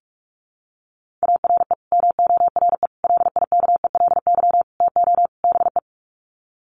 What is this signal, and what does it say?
Morse code, heard as a keyed tone, about 44 wpm, 700 Hz. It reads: ARE MORE LIKELY TO BE
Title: Sets of 5 Words 45wpm